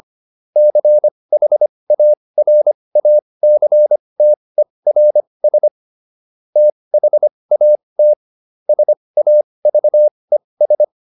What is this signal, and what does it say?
Morse code, heard as a keyed tone, about 25 wpm, spaced as for 20 wpm, 600 Hz. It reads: CHARACTERS THAT SAVES